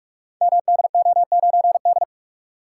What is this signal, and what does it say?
Morse code, heard as a keyed tone, about 44 wpm, 700 Hz. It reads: MDO9D
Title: Code Groups - 5 Letter:Numbers - 45wpm